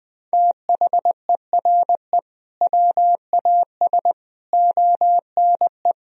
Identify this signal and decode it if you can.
Morse code, heard as a keyed tone, about 20 wpm, 700 Hz. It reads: THERE WAS ONE
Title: Sentences from Top 1000 Words - Review Part 3 - 20wpm